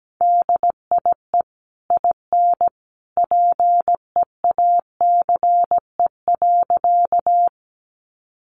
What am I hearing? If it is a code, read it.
Morse code, heard as a keyed tone, about 17 wpm, 700 Hz. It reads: DIE IN PEACE.
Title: Sentences from Top 1500 Words 17wpm